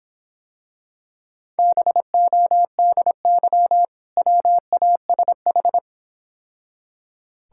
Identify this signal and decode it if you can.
Morse code, heard as a keyed tone, about 26 wpm, 700 Hz. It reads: BODY WASH